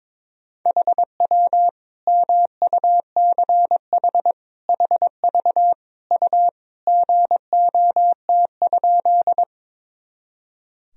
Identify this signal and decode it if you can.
Morse code, heard as a keyed tone, about 22 wpm, 700 Hz. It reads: HW MUCH HV U GOT?
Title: Sentences from Top 300 Words and Abbreviations 22wpm